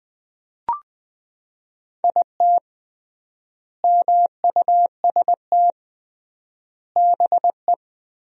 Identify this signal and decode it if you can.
Morse code, heard as a keyed tone, about 20 wpm, 700 Hz. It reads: IT MUST BE
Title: 3x Word-Spacing - Sentences from Top 500 Words - 20wpm